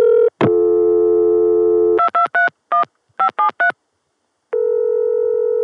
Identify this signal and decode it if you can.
DTMF, heard as telephone touch tones, tones of about 120 ms. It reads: 2231 573